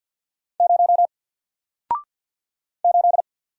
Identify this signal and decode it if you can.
Morse code, heard as a keyed tone, about 48 wpm, 700 Hz. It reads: 0 8
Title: Single Letter-Number 50wpm